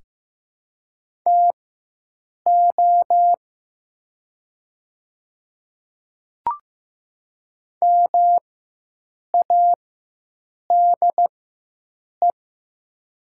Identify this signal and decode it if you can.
Morse code, heard as a keyed tone, about 15 wpm, spaced as for 7 wpm, 700 Hz. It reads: TO MADE
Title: 4x Character Spacing - Top 200 Words - 15wpm